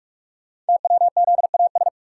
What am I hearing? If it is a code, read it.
Morse code, heard as a keyed tone, about 44 wpm, 690 Hz. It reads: TWZAS
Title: Code Groups - 5 Letters - 45wpm